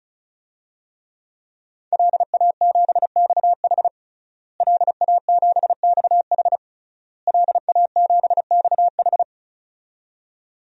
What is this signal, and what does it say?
Morse code, heard as a keyed tone, about 36 wpm, 700 Hz. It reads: LA7XH LA7XH LA7XH
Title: DXCC-100 Call Signs - 5 Characters - Repeated 3x - 35wpm